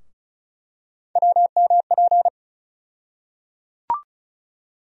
Morse code, heard as a keyed tone, about 36 wpm, 700 Hz. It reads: WMP
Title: USA Call Sign Postfixes - 3 Letters 35wpm